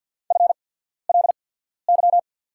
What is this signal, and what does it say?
Morse code, heard as a keyed tone, about 48 wpm, spaced as for 14 wpm, 690 Hz. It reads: FLY
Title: 8x Character Spacing - Top 500 Words - 50wpm